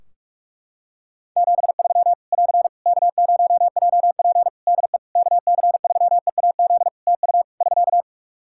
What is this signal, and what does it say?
Morse code, heard as a keyed tone, about 46 wpm, 700 Hz. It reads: 73 <AR> K0JP DE KC3EAZ TU <SK>
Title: Practice QSO - Exchange 4 - Ending - 45wpm